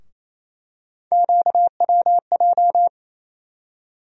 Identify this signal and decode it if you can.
Morse code, heard as a keyed tone, about 28 wpm, 700 Hz. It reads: QWJ
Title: USA Call Sign Postfixes - 3 Letters 28wpm